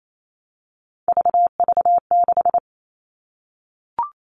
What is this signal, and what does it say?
Morse code, heard as a keyed tone, about 28 wpm, 700 Hz. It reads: VV6
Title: All Cut Numbers - 3-Digit 28wpm